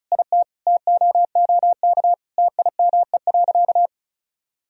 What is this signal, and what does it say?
Morse code, heard as a keyed tone, about 36 wpm, 700 Hz. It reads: IT TOOK TIME.
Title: Sentences from Top 1000 Words - Review Part 4 - 35wpm